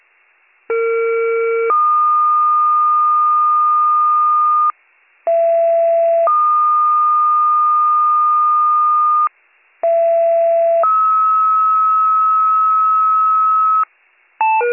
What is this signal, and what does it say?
Two-tone page, heard as a one-second tone, then a three-second tone, about 460 then 1220 Hz, about 670 then 1220 Hz, about 670 then 1290 Hz. It is heard from 0.7 to 13.8 s.